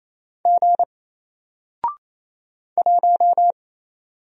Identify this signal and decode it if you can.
Morse code, heard as a keyed tone, about 28 wpm, 700 Hz. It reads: G 1
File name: Single-Letter-Number-28wpm